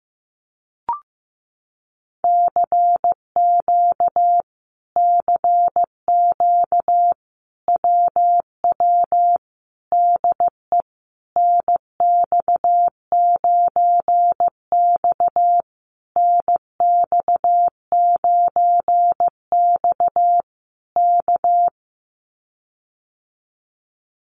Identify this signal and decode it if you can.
Morse code, heard as a keyed tone, about 15 wpm, 700 Hz. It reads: CQ CQ WW DE NX9X NX9X K
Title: Common QSO Phrases Encore 15wpm